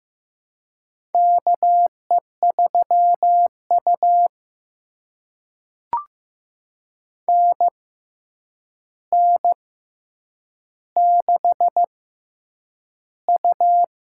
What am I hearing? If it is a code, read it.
Morse code, heard as a keyed tone, about 15 wpm, spaced as for 8 wpm, 700 Hz. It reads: KE3U NN6U
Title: USA Call Signs - 4 Characters - 6x Character Spacing - 15wpm